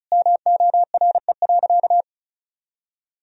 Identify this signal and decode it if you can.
Morse code, heard as a keyed tone, about 36 wpm, 700 Hz. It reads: MORE.
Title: Sentences from Top 1000 Words - Review Part 4 - 35wpm